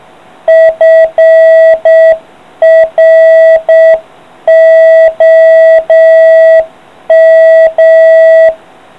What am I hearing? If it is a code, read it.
Morse code, heard as a keyed tone, about 7 wpm, 660 Hz. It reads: FROM